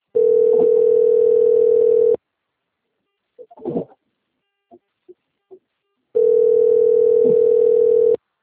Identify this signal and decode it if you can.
Ringback tone (North America), 440 + 480 Hz.